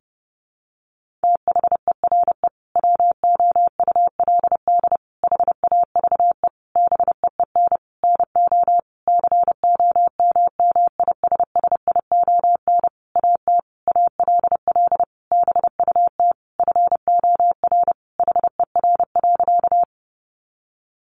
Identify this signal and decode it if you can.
Morse code, heard as a keyed tone, about 30 wpm, 700 Hz. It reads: THERE WOULD HAVE BEEN NO COMMISSION AT ALL BUT FOR HER.
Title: Sentences from Top 1400 Words 30wpm